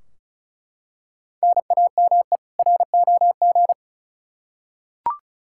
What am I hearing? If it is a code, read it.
Morse code, heard as a keyed tone, about 36 wpm, 700 Hz. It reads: NAME ROG
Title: Practice QSO - Exchange 1 - Beginning - 35wpm